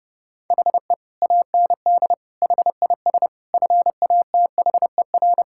Morse code, heard as a keyed tone, about 30 wpm, 700 Hz. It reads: HE AND HIS FATHER
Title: Sentences from Top 1600 Words 30wpm